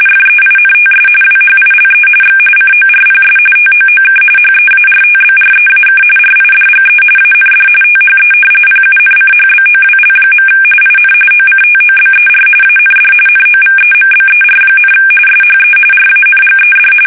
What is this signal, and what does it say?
RTTY, about 45.45 baud. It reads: FEINTS WHILE X CORPS TOOK HEARTBREAK RIDGE AND THE PUNCHBOWL. IN OCTOBER AND NOVEMBER, SHE RANGED
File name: fsk850 / battleshipnj